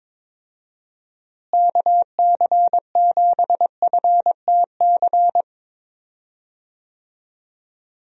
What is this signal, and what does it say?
Morse code, heard as a keyed tone, about 22 wpm, 700 Hz. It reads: KC7FTC
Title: USA Call Signs - 6 Characters - Repeated 3x - 22wpm